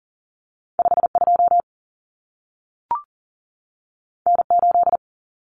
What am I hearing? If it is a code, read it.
Morse code, heard as a keyed tone, about 40 wpm, 700 Hz. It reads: 52 N8
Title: Common Cut Numbers - 2-Digit 40wpm